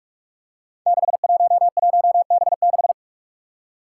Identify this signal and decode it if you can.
Morse code, heard as a keyed tone, about 46 wpm, 700 Hz. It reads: 611B6
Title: Five Challenging Letter-Number Combinations 45wpm